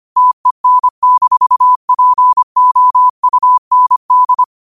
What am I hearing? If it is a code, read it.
Morse code, heard as a keyed tone, about 25 wpm, 1000 Hz. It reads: TEN-POUND
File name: qotdfiles / 25wpm